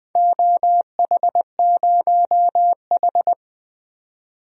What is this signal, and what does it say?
Morse code, heard as a keyed tone, about 20 wpm, 700 Hz. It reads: OH0H
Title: DXCC-100 Call Signs - 4 Characters - 20wpm